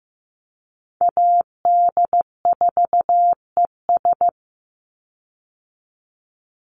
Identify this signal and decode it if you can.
Morse code, heard as a keyed tone, about 15 wpm, 700 Hz. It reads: AD4ES
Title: Call Signs Repeated 3x - 15wpm